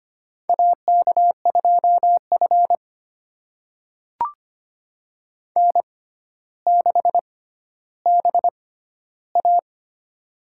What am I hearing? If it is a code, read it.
Morse code, heard as a keyed tone, about 25 wpm, spaced as for 13 wpm, 700 Hz. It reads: AK2F N6BA
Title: USA Call Signs - 4 Characters - 6x Character Spacing - 25wpm